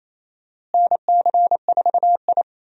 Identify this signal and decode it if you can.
Morse code, heard as a keyed tone, about 28 wpm, 700 Hz. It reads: NC4I
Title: USA Call Signs - 4 Characters - Repeated 2x - 28wpm